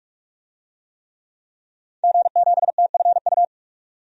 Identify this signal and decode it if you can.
Morse code, heard as a keyed tone, about 46 wpm, 700 Hz. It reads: G7TFU
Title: DXCC-100 Call Signs - 5 Characters - 45wpm